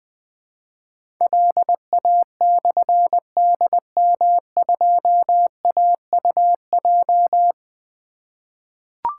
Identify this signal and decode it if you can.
Morse code, heard as a keyed tone, about 21 wpm, 700 Hz. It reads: LA/DM2AUJ
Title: Difficult Call Signs 20wpm